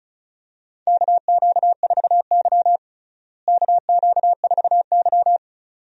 Morse code, heard as a keyed tone, about 36 wpm, 700 Hz. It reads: KQ4Y KQ4Y
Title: USA Call Signs - 4 Characters - Repeated 2x - 35wpm